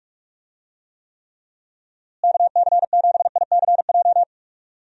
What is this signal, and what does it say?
Morse code, heard as a keyed tone, about 42 wpm, 700 Hz. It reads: KC7ICJ
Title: USA Call Signs - 6 Characters - Repeated 3x - 45wpm